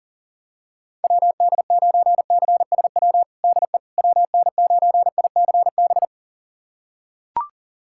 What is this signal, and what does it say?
Morse code, heard as a keyed tone, about 40 wpm, 700 Hz. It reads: WD9CSW DE WN9ICB
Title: USA Callsign DE Callsign 40wpm